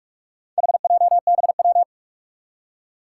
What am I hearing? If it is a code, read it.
Morse code, heard as a keyed tone, about 44 wpm, 700 Hz. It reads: HJBW
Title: Four Challenging Letter-Number Combinations 45wpm